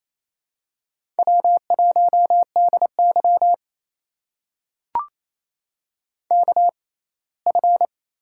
Morse code, heard as a keyed tone, about 28 wpm, spaced as for 10 wpm, 700 Hz. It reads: W1DYK F
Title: USA Call Signs - 6x Character Spacing - 28wpm